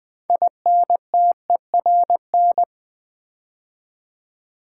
Morse code, heard as a keyed tone, about 20 wpm, 700 Hz. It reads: INTERN
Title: Build A Word - 2 Letters At A Time - 8-Letter Words - 20wpm